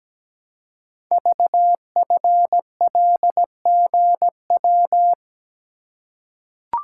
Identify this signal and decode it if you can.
Morse code, heard as a keyed tone, about 17 wpm, 700 Hz. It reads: VFLGW
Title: Code Groups - 5 Letters - 17wpm